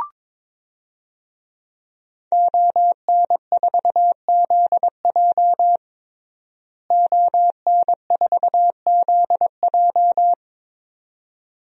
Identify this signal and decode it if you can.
Morse code, heard as a keyed tone, about 22 wpm, 700 Hz. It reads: ON4ZJ ON4ZJ
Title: DXCC-100 Call Signs - 5 Characters - Repeated 2x - 22wpm